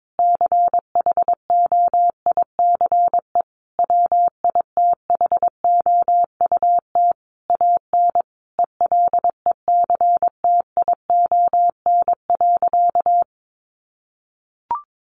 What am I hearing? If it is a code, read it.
Morse code, heard as a keyed tone, about 22 wpm, 700 Hz. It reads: CHOICE WITHOUT AN ELECTION.
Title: Sentences from Top 1500 Words 22wpm